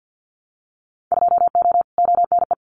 Morse code, heard as a keyed tone, about 50 wpm, 700 Hz. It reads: 2O ONE
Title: Winter Field Day Exchanges Encore 50wpm